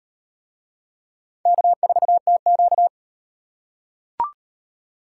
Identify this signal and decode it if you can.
Morse code, heard as a keyed tone, about 38 wpm, 700 Hz. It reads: K4TQ